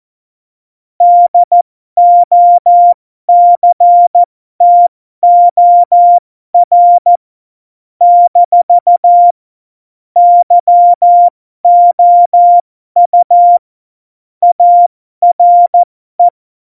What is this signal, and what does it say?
Morse code, heard as a keyed tone, about 14 wpm, spaced as for 12 wpm, 700 Hz. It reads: DOCTOR - YOU ARE
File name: Morsekurs / Witze